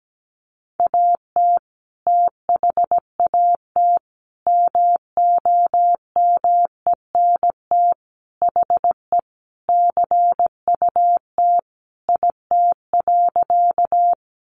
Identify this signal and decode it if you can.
Morse code, heard as a keyed tone, about 17 wpm, 700 Hz. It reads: AT THAT MOMENT HE CUT IT.
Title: Sentences from Top 1000 Words 17wpm